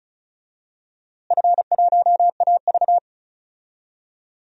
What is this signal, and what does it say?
Morse code, heard as a keyed tone, about 36 wpm, 700 Hz. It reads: F1AV
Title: DXCC-100 Call Signs - 4 Characters - Repeated 3x - 35wpm